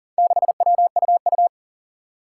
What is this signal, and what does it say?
Morse code, heard as a keyed tone, about 40 wpm, 700 Hz. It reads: 6WUU
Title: Four Challenging Letter-Number Combinations 40wpm